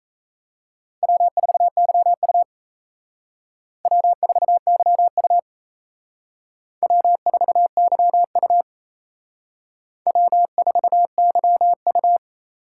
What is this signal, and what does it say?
Morse code, heard as a keyed tone, about 35 wpm, 700 Hz. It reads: W4YU W4YU W4YU W4YU